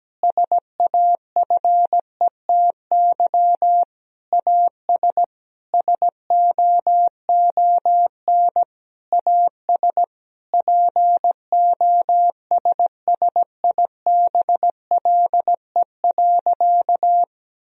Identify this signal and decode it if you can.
Morse code, heard as a keyed tone, about 17 wpm, 700 Hz. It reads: SAFETY AS SOON AS POSSIBLE.
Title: Sentences from Top 1000 Words - Review Part 4 - 17wpm